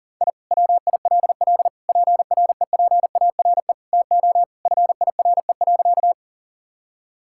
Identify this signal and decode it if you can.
Morse code, heard as a keyed tone, about 37 wpm, 700 Hz. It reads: I WILL PREPARE TO FIRE.
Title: Sentences from Top 1000 Words - Review Part 6 - 40wpm